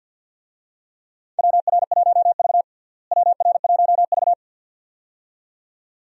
Morse code, heard as a keyed tone, about 51 wpm, 700 Hz. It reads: WR1V WR1V
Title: USA Call Signs - 4 Characters - Repeated 2x - 50wpm